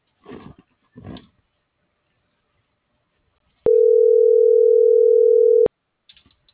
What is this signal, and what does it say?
Ringback tone (North America), 440 + 480 Hz.